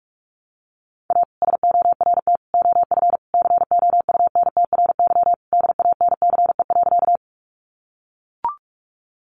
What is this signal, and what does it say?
Morse code, heard as a keyed tone, about 44 wpm, 700 Hz. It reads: A SORT OF COUNTRY DANCE.
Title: Sentences from Top 1300 Words 45wpm